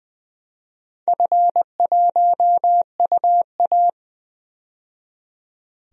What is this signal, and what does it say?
Morse code, heard as a keyed tone, about 21 wpm, 700 Hz. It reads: F1UA
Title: DXCC-100 Call Signs - 4 Characters - 20wpm